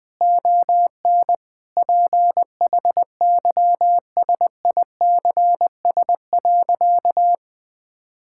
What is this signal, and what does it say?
Morse code, heard as a keyed tone, about 21 wpm, 700 Hz. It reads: ON PHYSICS.
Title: Sentences from Top 1000 Words - Review Part 5 - 20wpm